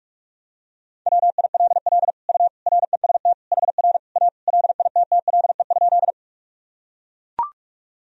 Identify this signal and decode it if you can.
Morse code, heard as a keyed tone, about 42 wpm, 700 Hz. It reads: WILL U REST HR A LITTLE?
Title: Sentences from Top 500 Words and Abbreviations 45wpm